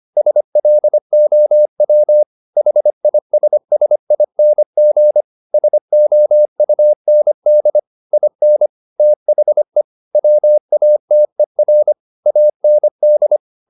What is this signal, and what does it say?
Morse code, heard as a keyed tone, about 25 wpm, 600 Hz. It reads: SLOW HISSING SOUND IN THE WATER AND
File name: the old man and the sea part 1 25wpm